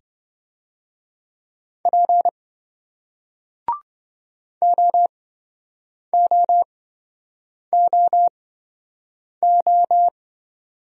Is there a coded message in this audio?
Yes: Morse code from 1.9 to 10.1 s, heard as a keyed tone, about 28 wpm, spaced as for 7 wpm, 700 Hz. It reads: P OOOO